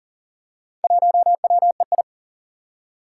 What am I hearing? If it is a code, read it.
Morse code, heard as a keyed tone, about 40 wpm, 700 Hz. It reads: 1WEI
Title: Four Challenging Letter-Number Combinations 40wpm